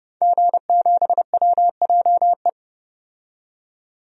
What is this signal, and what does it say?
Morse code, heard as a keyed tone, about 30 wpm, 700 Hz. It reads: G7WJE
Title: DXCC-100 Call Signs - 5 Characters - 30wpm